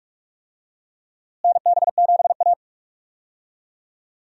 Morse code, heard as a keyed tone, about 42 wpm, 700 Hz. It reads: NB7A
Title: USA Call Signs - 4 Characters - Repeated 2x - 45wpm